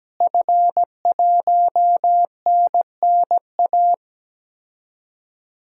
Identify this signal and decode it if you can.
Morse code, heard as a keyed tone, about 17 wpm, 700 Hz. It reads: F1NNA
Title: DXCC-100 Call Signs - 5 Characters - Repeated 3x - 17wpm